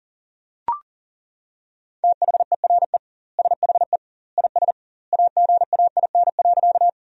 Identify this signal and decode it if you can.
Morse code, heard as a keyed tone, about 40 wpm, 700 Hz. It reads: THERE SHE IS AGAIN.
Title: 2x Word-Spacing - Sentences from Top 200 Words - 40wpm